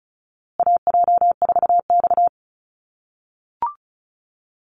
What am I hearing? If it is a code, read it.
Morse code, heard as a keyed tone, about 34 wpm, 700 Hz. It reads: AJ4X
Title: USA Call Signs Repeated 2x 35wpm